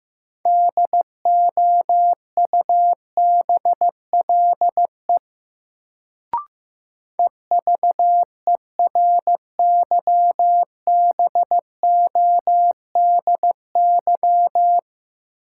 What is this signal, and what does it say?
Morse code, heard as a keyed tone, about 15 wpm, 700 Hz. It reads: DOUBLE EVERYBODY
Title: Sentences from Top 1400 Words - New Word Drill - 15wpm